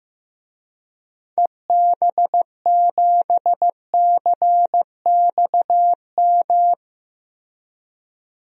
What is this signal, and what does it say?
Morse code, heard as a keyed tone, about 15 wpm, 700 Hz. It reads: EB7CXM
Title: DXCC-100 Call Signs - 6 Characters - Repeated 2x - 15wpm